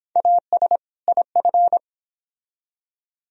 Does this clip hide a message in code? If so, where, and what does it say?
Yes: Morse code from 0.2 to 1.8 s, heard as a keyed tone, about 30 wpm, 700 Hz. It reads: AS IF